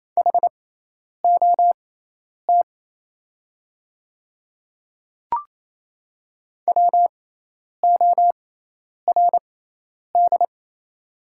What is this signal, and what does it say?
Morse code, heard as a keyed tone, about 28 wpm, spaced as for 10 wpm, 690 Hz. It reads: HOT WORD
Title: 6x Character Spacing - Top 100 Words - 28wpm